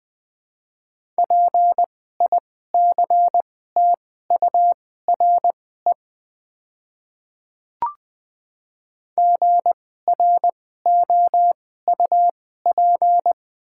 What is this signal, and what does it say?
Morse code, heard as a keyed tone, about 20 wpm, spaced as for 14 wpm, 700 Hz. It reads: PICTURE GROUP
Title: 2x Character Spacing - Top 300 Words - 20wpm